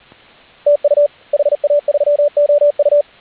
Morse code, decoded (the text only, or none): TU HA3OU